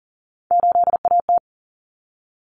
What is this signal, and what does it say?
Morse code, heard as a keyed tone, about 40 wpm, 700 Hz. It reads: 8AT
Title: Common Cut Numbers - 3-Digit 40wpm